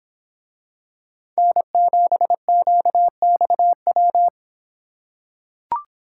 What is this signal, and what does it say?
Morse code, heard as a keyed tone, about 26 wpm, 700 Hz. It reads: N7QXW